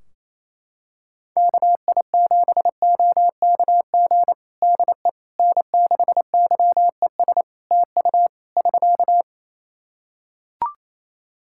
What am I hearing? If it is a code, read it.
Morse code, heard as a keyed tone, about 27 wpm, 700 Hz. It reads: KI7OKG DE N6YES TU <SK>
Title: Practice QSO - Exchange 4 - Ending - 28wpm